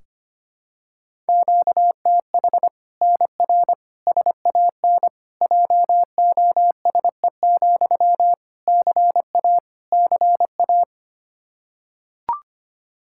Morse code, heard as a keyed tone, about 25 wpm, 700 Hz. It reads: QTH NR SAN JOSE, CA CA
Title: Practice QSO - Exchange 1 - Beginning - 25wpm